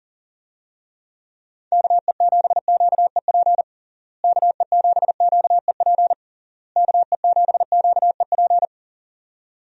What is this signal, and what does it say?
Morse code, heard as a keyed tone, about 40 wpm, 700 Hz. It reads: KE7QEP KE7QEP KE7QEP
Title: USA Call Signs - 6 Characters - Repeated 3x - 40wpm